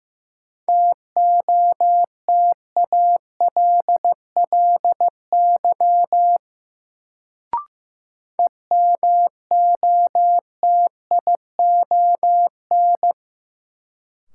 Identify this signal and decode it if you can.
Morse code, heard as a keyed tone, about 16 wpm, 700 Hz. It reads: TOTALLY EMOTION
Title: Sentences from Top 1600 Words - New Word Drill Alt - 15wpm